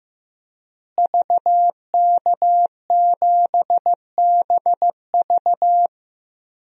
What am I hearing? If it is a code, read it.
Morse code, heard as a keyed tone, about 16 wpm, 700 Hz. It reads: VK7BV
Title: DXCC-100 Call Signs - 5 Characters - Repeated 2x - 15wpm